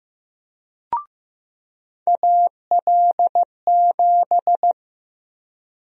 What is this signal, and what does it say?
Morse code, heard as a keyed tone, about 15 wpm, 700 Hz. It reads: AL7
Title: USA Call Sign Prefixes - 15wpm